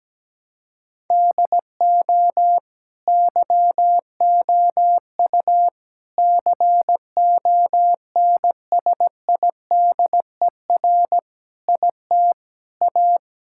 Morse code, heard as a keyed tone, about 17 wpm, 700 Hz. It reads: DO YOU CONSIDER IT A
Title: Sentences from Top 1000 Words - Review Part 6 - 17wpm